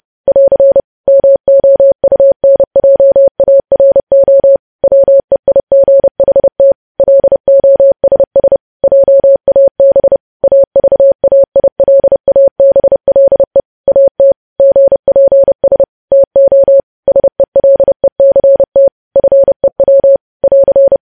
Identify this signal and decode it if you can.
Morse code, heard as a keyed tone, about 30 wpm, 550 Hz. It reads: <AR> MOUNJARO WEIGHT LOSS JAB AVAILABLE AT GPS TO SELECT FEW <AR>
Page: news headlines in morse code at 30 wpm